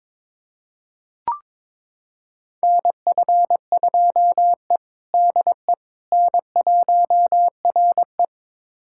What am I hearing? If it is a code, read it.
Morse code, heard as a keyed tone, about 22 wpm, 700 Hz. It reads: NF2E DE N1RE
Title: USA Callsign DE Callsign - 4 Characters - 22wpm